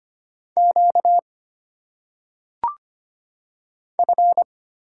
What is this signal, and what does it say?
Morse code, heard as a keyed tone, about 25 wpm, 700 Hz. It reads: Q F
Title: Single Letters - Phonetic Pronunciation 25wpm